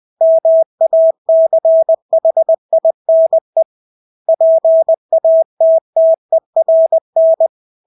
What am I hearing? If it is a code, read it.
Morse code, heard as a keyed tone, about 20 wpm, 650 Hz. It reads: MACHINE PATTERN
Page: Engelske ord 20 wpm